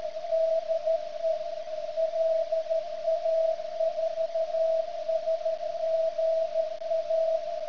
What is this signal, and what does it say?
Morse code, heard as a keyed tone, about 13 wpm, 650 Hz. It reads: DE LA4SGA
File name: ft8-vs-cw / 75-to-150 CW 100hz